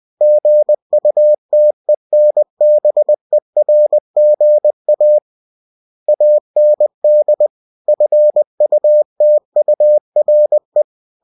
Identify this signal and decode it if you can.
Morse code, heard as a keyed tone, about 20 wpm, 600 Hz. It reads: GUTENBERGA AND FUTURE
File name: cw-books / winnie-the-pooh / winnie-20wpm-page-0120